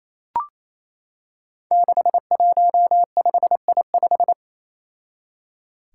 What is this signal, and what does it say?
Morse code, heard as a keyed tone, about 28 wpm, 700 Hz. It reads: 615I5
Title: Five Challenging Letter-Number Combinations 28wpm